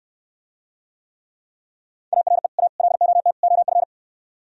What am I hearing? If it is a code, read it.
Morse code, heard as a keyed tone, about 68 wpm, 700 Hz. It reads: ARE A LOT OF